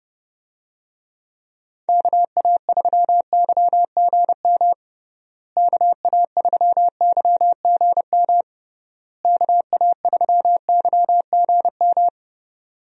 Morse code, heard as a keyed tone, about 30 wpm, 700 Hz. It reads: KA3YGM KA3YGM KA3YGM
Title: USA Call Signs - 6 Characters - Repeated 3x - 30wpm